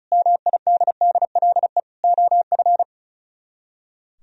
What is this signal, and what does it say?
Morse code, heard as a keyed tone, about 43 wpm, 700 Hz. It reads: MIDDLE OF
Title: Speed-Racing - Sets of 3 Words 35wpm